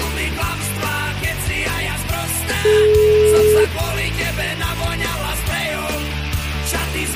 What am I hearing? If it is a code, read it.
Ringback tone (Europe, 425 Hz), with noise.